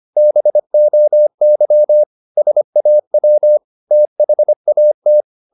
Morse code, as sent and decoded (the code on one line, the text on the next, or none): -... --- -.-- / ... .- .-- / - .... .- -
BOY SAW THAT